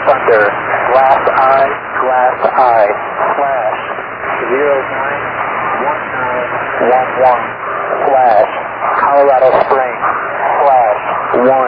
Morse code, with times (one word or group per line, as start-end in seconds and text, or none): none